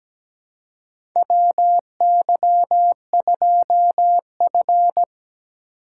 Morse code, heard as a keyed tone, about 17 wpm, 700 Hz. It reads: WY2F
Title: USA Call Signs - 4 Characters - 3x Character Spacing - 17wpm